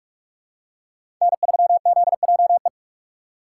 Morse code, heard as a keyed tone, about 46 wpm, 700 Hz. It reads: N3ZJE
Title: USA Call Signs - 5 Characters - 2x Character Spacing - 45wpm